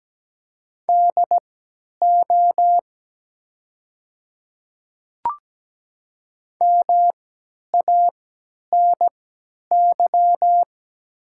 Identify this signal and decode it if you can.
Morse code, heard as a keyed tone, about 17 wpm, spaced as for 10 wpm, 700 Hz. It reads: DO MANY
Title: 3x Character Spacing - Top 300 Words - 17wpm